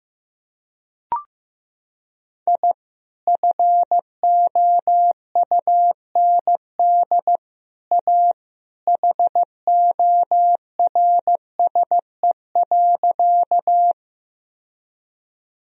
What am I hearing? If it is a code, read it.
Morse code, heard as a keyed tone, about 16 wpm, 700 Hz. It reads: I FOUND A HORSE.
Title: Sentences from Top 1000 Words - Review Part 1 - 15wpm